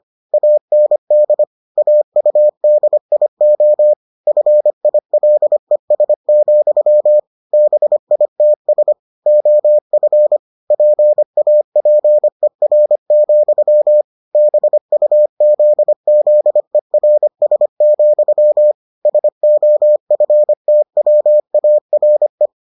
Morse code, heard as a keyed tone, about 25 wpm, 600 Hz. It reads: AND AUDIO FILES, BITS OF PAPER, BUZZERS, SOFTWARE